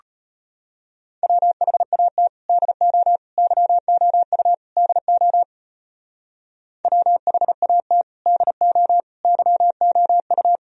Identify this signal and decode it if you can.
Morse code, heard as a keyed tone, about 32 wpm, 700 Hz. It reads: WHAT DO YOU DO WHAT DO YOU